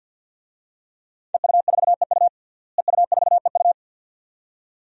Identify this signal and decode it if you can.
Morse code, heard as a keyed tone, about 46 wpm, 700 Hz. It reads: EU4EU EU4EU
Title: DXCC-100 Call Signs - 5 Characters - Repeated 2x - 50wpm